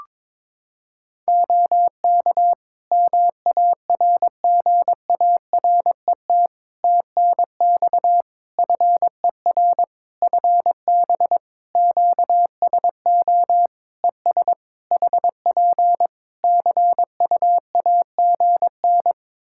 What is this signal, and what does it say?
Morse code, heard as a keyed tone, about 22 wpm, 700 Hz. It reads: OK MARGARET TNX FER FB QSO ES HP CUAGN
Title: Practice QSO - Exchange 4 - Ending - 22wpm